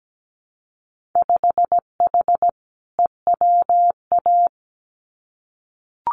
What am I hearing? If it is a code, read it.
Morse code, heard as a keyed tone, about 17 wpm, 700 Hz. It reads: 5H EWA
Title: Winter Field Day Exchanges Encore 17wpm